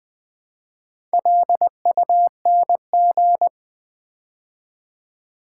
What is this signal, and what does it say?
Morse code, heard as a keyed tone, about 20 wpm, 700 Hz. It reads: LUNG
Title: Build A Word - 2 Letters At A Time - 4-Letter Words - 20wpm